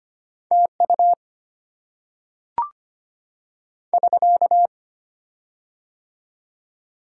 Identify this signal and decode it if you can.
Morse code, heard as a keyed tone, about 25 wpm, 700 Hz. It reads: TU <SK>
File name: Top 90 QSO Elements - 6x Character Spacing - 25wpm